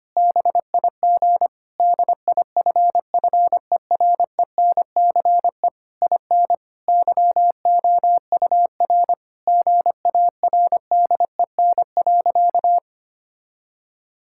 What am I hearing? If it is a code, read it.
Morse code, heard as a keyed tone, about 25 wpm, 700 Hz. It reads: BIG DIFFERENCE IN YOUR GARDEN.
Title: Sentences from Top 1000 Words - Review Part 5 - 25wpm